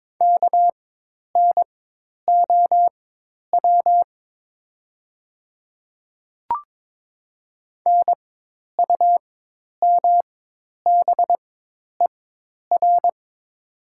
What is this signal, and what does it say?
Morse code, heard as a keyed tone, about 22 wpm, spaced as for 10 wpm, 700 Hz. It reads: KNOW NUMBER
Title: 4x Character Spacing - Top 100 Words - 22wpm